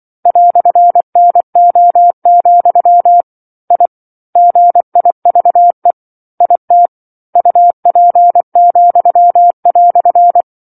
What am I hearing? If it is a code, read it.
Morse code, heard as a keyed tone, about 24 wpm, 700 Hz. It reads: "NO, I GIVE IT UP,"
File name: 07-Chapter-07 24 WPM